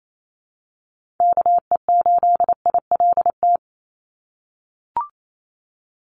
Morse code, heard as a keyed tone, about 28 wpm, 700 Hz. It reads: KE8ILT
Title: USA Call Signs 28wpm